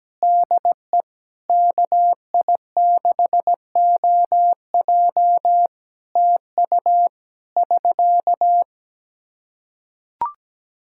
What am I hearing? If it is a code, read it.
Morse code, heard as a keyed tone, about 17 wpm, 700 Hz. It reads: DE KI6OJ TU <SK>
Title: K1USN SST QSO Practice 17wpm